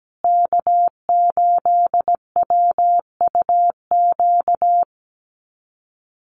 Morse code, heard as a keyed tone, about 17 wpm, 700 Hz. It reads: K8WUQ
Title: USA Call Signs Repeated 3x 17wpm